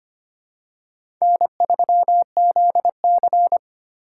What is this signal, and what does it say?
Morse code, heard as a keyed tone, about 25 wpm, 700 Hz. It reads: N3ZC
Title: USA Call Signs - 4 Characters - 8x Character Spacing - 25wpm